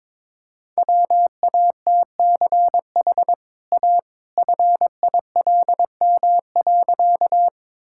Morse code, heard as a keyed tone, about 22 wpm, 690 Hz. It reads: WATCH A FILM.
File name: practice / 22wpm / Easy Sentences with Hints - 22wpm